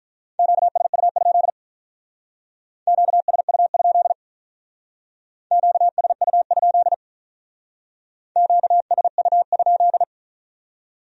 Morse code, heard as a keyed tone, about 44 wpm, 700 Hz. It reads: QSU? QSU? QSU? QSU?